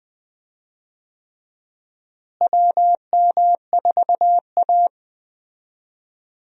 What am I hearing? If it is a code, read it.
Morse code, heard as a keyed tone, about 20 wpm, 700 Hz. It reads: WM4A
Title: USA Call Signs - 4 Characters - Repeated 3x - 20wpm